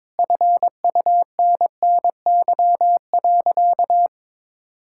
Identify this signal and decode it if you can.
Morse code, heard as a keyed tone, about 21 wpm, 700 Hz. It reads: FUNNY.
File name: practice / 22wpm / Sentences with 4-Word Phrases - 22wpm